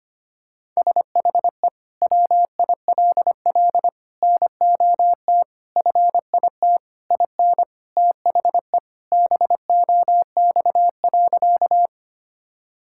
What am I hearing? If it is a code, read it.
Morse code, heard as a keyed tone, about 25 wpm, 700 Hz. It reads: SHE WILL NOT FIT IN THE BOX.
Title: Sentences from Top 1000 Words - Review Part 5 - 25wpm